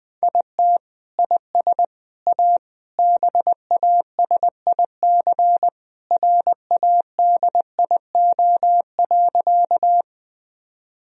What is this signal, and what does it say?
Morse code, heard as a keyed tone, about 19 wpm, 700 Hz. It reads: IT IS A BASIC RADIO.
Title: Sentences from Top 1000 Words - Review Part 3 - 20wpm